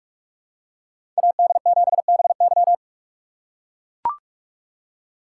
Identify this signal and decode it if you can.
Morse code, heard as a keyed tone, about 41 wpm, 700 Hz. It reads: AD7BY
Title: USA Call Signs - 3x Character Spacing - 45wpm